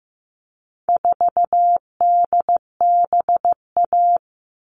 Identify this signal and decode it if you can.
Morse code, heard as a keyed tone, about 15 wpm, 700 Hz. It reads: 4DBA
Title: All Cut Numbers - 4-Digit 15wpm